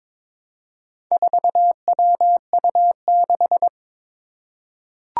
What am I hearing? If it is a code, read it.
Morse code, heard as a keyed tone, about 22 wpm, 700 Hz. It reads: 4WU6
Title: Four Challenging Letter-Number Combinations 22wpm